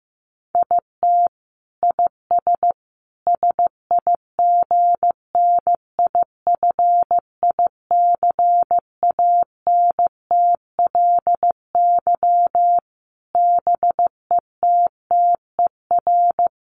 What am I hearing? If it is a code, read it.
Morse code, heard as a keyed tone, about 15 wpm, 700 Hz. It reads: IT IS SIGNIFICANTLY BETTER
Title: Sentences from Top 1500 Words 15wpm